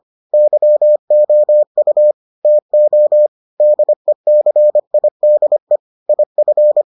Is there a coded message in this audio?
Yes: Morse code from 0.3 to 6.8 s, heard as a keyed tone, about 25 wpm, 600 Hz. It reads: YOU TO DECIDE IF